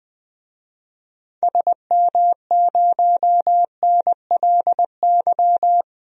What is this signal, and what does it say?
Morse code, heard as a keyed tone, about 20 wpm, 700 Hz. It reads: SM0NLY
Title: DXCC-100 Call Signs - 6 Characters - 20wpm